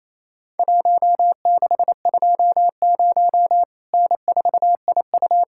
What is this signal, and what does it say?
Morse code, heard as a keyed tone, about 28 wpm, 700 Hz. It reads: 1620 N4IU
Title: Worked All Europe DX Contest - QTC Exchange - 28wpm